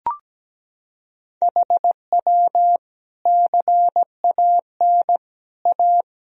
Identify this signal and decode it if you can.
Morse code, heard as a keyed tone, about 17 wpm, 700 Hz. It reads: HW CAN A
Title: Sentences from Top 500 Words and Abbreviations 17wpm